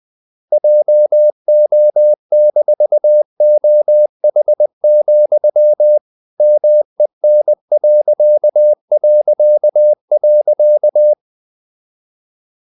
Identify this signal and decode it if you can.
Morse code, heard as a keyed tone, about 20 wpm, 600 Hz. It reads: JO-OH, MEN...
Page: Ferie 20 wpm